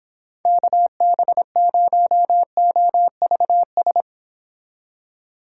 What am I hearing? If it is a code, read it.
Morse code, heard as a keyed tone, about 26 wpm, 700 Hz. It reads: KB0OVS